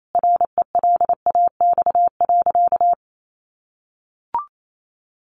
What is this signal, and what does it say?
Morse code, heard as a keyed tone, about 28 wpm, 700 Hz. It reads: RELAX.
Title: Sentences from Top 1400 Words 28wpm